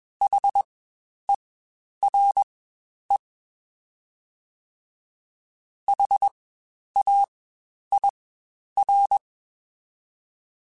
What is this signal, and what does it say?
Morse code, heard as a keyed tone, about 21 wpm, spaced as for 10 wpm, 800 Hz. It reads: HERE HAIR
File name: Spaced Alphas / Spaced Alphas mp3s / Hs sp @20